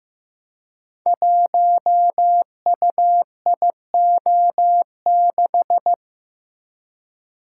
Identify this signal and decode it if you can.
Morse code, heard as a keyed tone, about 15 wpm, 700 Hz. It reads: 1UIO6
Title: Code Groups - 5 Letter:Numbers - 15wpm